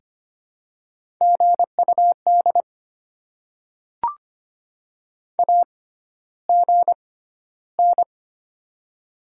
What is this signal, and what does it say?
Morse code, heard as a keyed tone, about 25 wpm, spaced as for 13 wpm, 700 Hz. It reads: GUD AGN
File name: Top 20 QSO Elements - 6x Character Spacing - 25wpm